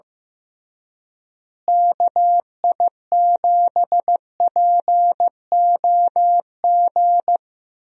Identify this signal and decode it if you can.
Morse code, heard as a keyed tone, about 16 wpm, 700 Hz. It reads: KI7POG
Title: USA Call Signs - 6 Characters - Repeated 3x - 15wpm